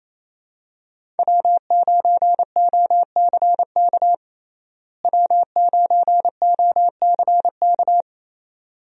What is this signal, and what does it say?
Morse code, heard as a keyed tone, about 28 wpm, 700 Hz. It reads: W9OCK W9OCK
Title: USA Call Signs - 5 Characters - Repeated 2x - 28wpm